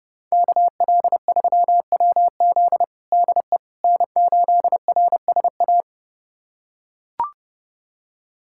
Morse code, heard as a keyed tone, about 32 wpm, 700 Hz. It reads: KL3WZ DE N8RSA
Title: USA Callsign DE Callsign - 5 Characters - 30wpm